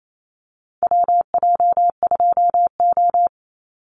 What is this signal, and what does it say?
Morse code, heard as a keyed tone, about 28 wpm, 700 Hz. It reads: WJ2O
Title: Call Signs - 28wpm